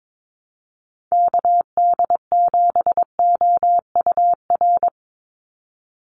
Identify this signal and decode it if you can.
Morse code, heard as a keyed tone, about 22 wpm, 700 Hz. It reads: KD7OUR
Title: USA Call Signs 22wpm